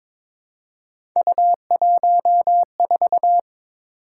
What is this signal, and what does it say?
Morse code, heard as a keyed tone, about 22 wpm, 700 Hz. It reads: U14
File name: practice / 22wpm / Three Letter-Number Combinations - 22wpm